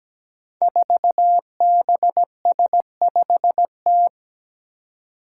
Morse code, heard as a keyed tone, about 17 wpm, 700 Hz. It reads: 4BS5T
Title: Five Challenging Letter-Number Combinations 17wpm